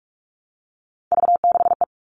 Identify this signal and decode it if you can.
Morse code, heard as a keyed tone, about 44 wpm, 700 Hz. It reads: V6E
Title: All Cut Numbers - 3-Digit 45wpm